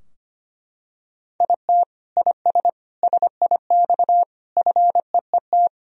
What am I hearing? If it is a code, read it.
Morse code, heard as a keyed tone, about 25 wpm, 700 Hz. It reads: IT IS SIX FEET
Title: Sentences from Top 1000 Words - Review Part 2 - 25wpm